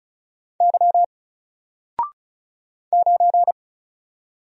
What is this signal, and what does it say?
Morse code, heard as a keyed tone, about 36 wpm, 700 Hz. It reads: Y 9
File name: Single-Letter-Number-35wpm